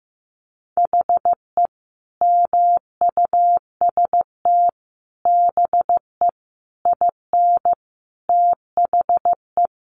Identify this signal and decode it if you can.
Morse code, heard as a keyed tone, about 15 wpm, 700 Hz. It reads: HE MUST BE IN THE
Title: Sentences from Top 500 Words 15wpm